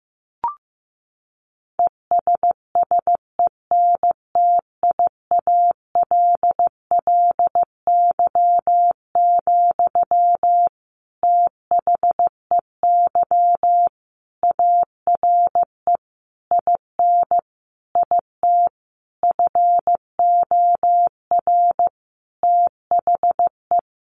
Morse code, heard as a keyed tone, about 15 wpm, 700 Hz. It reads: ESSENTIALLY, THEY ARE IN IT FOR THE
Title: Sentences from Top 1300 Words 15wpm